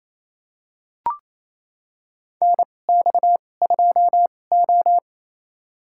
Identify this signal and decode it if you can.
Morse code, heard as a keyed tone, about 28 wpm, spaced as for 20 wpm, 700 Hz. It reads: NX2O
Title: USA Call Signs - 2x Character Spacing - 28wpm